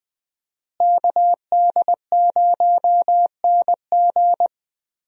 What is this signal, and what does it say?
Morse code, heard as a keyed tone, about 20 wpm, 700 Hz. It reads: KD0NG
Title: USA Call Signs - 5 Characters - Repeated 3x - 20wpm